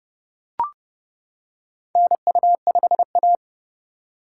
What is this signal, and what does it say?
Morse code, heard as a keyed tone, about 30 wpm, 700 Hz. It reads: NU5A
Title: K1USN SST QSO Practice - 30wpm